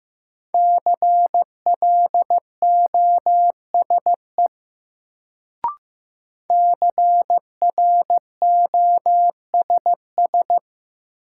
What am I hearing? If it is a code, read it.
Morse code, heard as a keyed tone, about 15 wpm, 700 Hz. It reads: CLOSE CROSS
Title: Sentences from Top 300 Words - New Word Drill - 15wpm